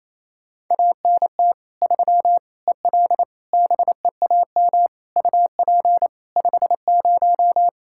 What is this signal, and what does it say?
Morse code, heard as a keyed tone, about 28 wpm, 700 Hz. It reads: ANT 3 EL BEAM UP 50
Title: Practice QSO - Exchange 2 - Info - 28wpm